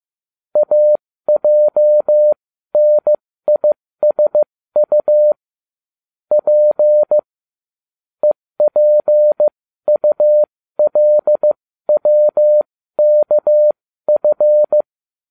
Morse code, heard as a keyed tone, about 15 wpm, spaced as for 13 wpm, 600 Hz. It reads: AJNISU P EPULWKF